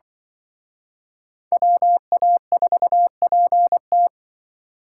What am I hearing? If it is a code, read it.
Morse code, heard as a keyed tone, about 24 wpm, 700 Hz. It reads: WA4PT
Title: Speed Racing - USA 2x2 Call Signs 20wpm